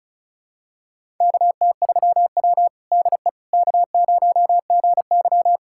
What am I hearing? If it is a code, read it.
Morse code, heard as a keyed tone, about 36 wpm, 700 Hz. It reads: KT3W DE K0GY
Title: USA Callsign DE Callsign - 4 Characters - 35wpm